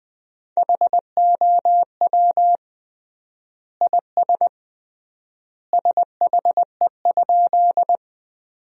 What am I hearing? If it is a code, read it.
Morse code, heard as a keyed tone, about 20 wpm, 700 Hz. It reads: HOW IS SHE?
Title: 3x Word-Spacing - Sentences from Top 100 Words - 20wpm